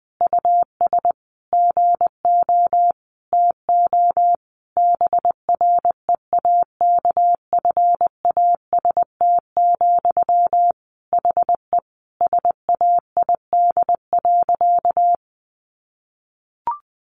Morse code, heard as a keyed tone, about 20 wpm, 700 Hz. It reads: US GO TO BREAKFAST, HE SAID.
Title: Sentences from Top 1500 Words 20wpm